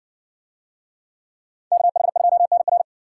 Morse code, heard as a keyed tone, about 59 wpm, 700 Hz. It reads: KH2NR